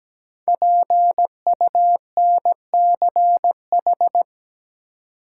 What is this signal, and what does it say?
Morse code, heard as a keyed tone, about 17 wpm, 700 Hz. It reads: PUNCH
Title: Five Letter Words - 17wpm